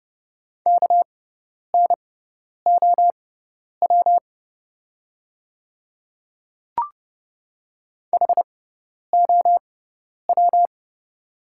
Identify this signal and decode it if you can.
Morse code, heard as a keyed tone, about 31 wpm, spaced as for 10 wpm, 700 Hz. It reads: KNOW HOW
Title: 6x Character Spacing - Top 100 Words - 30wpm